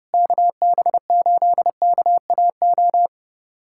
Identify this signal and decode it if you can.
Morse code, heard as a keyed tone, about 30 wpm, 700 Hz. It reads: KB8KAO
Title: USA Callsign DE Callsign 30wpm